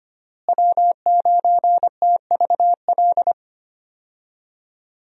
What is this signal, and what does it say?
Morse code, heard as a keyed tone, about 25 wpm, 700 Hz. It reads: W9TVL
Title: USA Call Signs - 5 Characters - Repeated 2x - 25wpm